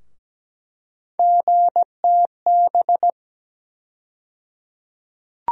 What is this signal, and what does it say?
Morse code, heard as a keyed tone, about 17 wpm, 700 Hz. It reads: GTB
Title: USA Call Sign Postfixes - 3 Letters 17wpm